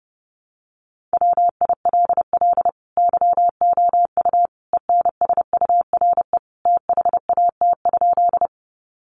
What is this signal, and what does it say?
Morse code, heard as a keyed tone, about 30 wpm, 700 Hz. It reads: WILL YOU ENSURE THAT?
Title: Sentences from Top 1100 Words 30wpm